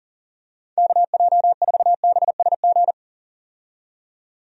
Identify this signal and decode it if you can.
Morse code, heard as a keyed tone, about 40 wpm, 700 Hz. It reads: KJ4BSG
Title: USA Call Signs - 6 Characters - 40wpm